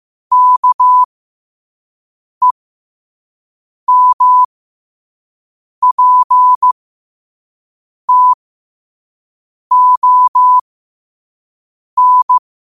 Morse code, heard as a keyed tone, about 15 wpm, spaced as for 5 wpm, 1000 Hz. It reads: KEMPTON
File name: qotdfiles / 5wpm